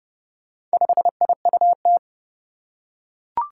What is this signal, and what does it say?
Morse code, heard as a keyed tone, about 32 wpm, 700 Hz. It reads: 5IUT
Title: Four Challenging Letter-Number Combinations 30wpm